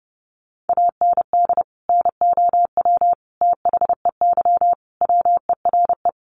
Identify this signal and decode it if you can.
Morse code, heard as a keyed tone, about 30 wpm, 700 Hz. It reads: AND NOW THEY WERE
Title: Sentences from Top 1100 Words 30wpm